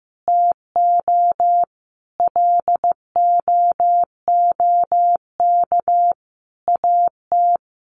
Morse code, heard as a keyed tone, about 15 wpm, 700 Hz. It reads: TO LOOK AT
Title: Sets of 5 Words Encore 15wpm